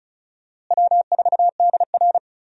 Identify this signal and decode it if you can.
Morse code, heard as a keyed tone, about 34 wpm, 700 Hz. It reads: W4DR
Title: USA Call Signs - 4 Characters - Repeated 2x - 35wpm